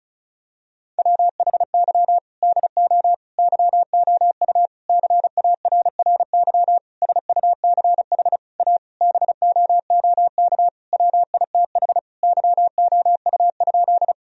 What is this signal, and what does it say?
Morse code, heard as a keyed tone, about 36 wpm, 700 Hz. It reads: WHY DO YOU CARRY SUCH A BOOK WITH YOU?
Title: Sentences from Top 1000 Words - Review Part 1 - 35wpm